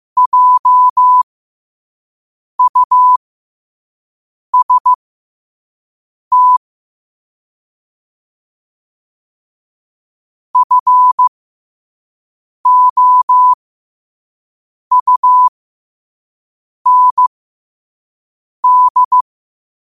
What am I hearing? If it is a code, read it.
Morse code, heard as a keyed tone, about 15 wpm, spaced as for 5 wpm, 1000 Hz. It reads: JUST FOUND
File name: qotdfiles / 5wpm